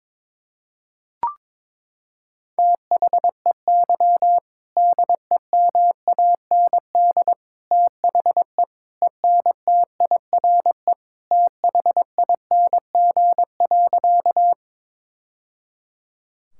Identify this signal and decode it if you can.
Morse code, heard as a keyed tone, about 22 wpm, 700 Hz. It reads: THEY DEMAND THE ENTIRE THING.
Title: Sentences from Top 1000 Words - Review Part 5 - 22wpm